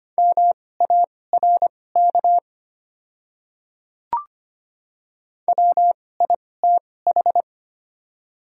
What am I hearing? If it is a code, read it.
Morse code, heard as a keyed tone, about 25 wpm, spaced as for 18 wpm, 700 Hz. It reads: MARK WITH
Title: 2x Character Spacing - Top 300 Words - 25wpm